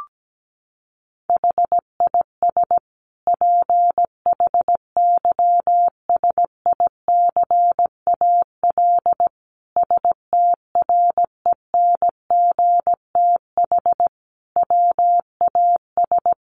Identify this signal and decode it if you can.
Morse code, heard as a keyed tone, about 17 wpm, 700 Hz. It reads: HIS PHYSICAL STRENGTH WAS
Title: Sentences from Top 1400 Words 17wpm